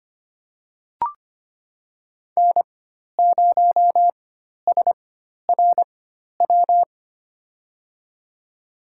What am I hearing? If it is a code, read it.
Morse code, heard as a keyed tone, about 25 wpm, spaced as for 12 wpm, 700 Hz. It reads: N0SRW
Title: USA Call Signs - 5 Characters - 4x Character Spacing - 25wpm